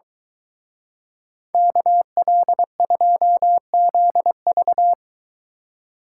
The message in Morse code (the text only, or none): KL2ZV